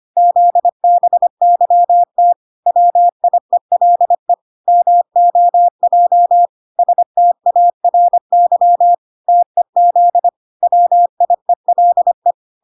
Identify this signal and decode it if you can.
Morse code, heard as a keyed tone, about 25 wpm, 700 Hz. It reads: ZBYT WIELE MOJ STARY TEZ WIELE